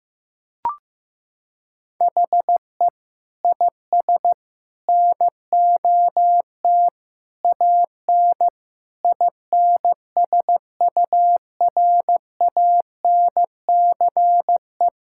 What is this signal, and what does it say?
Morse code, heard as a keyed tone, about 15 wpm, 700 Hz. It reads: HE IS NOT AN INSURANCE
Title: Sentences from Top 1600 Words 15wpm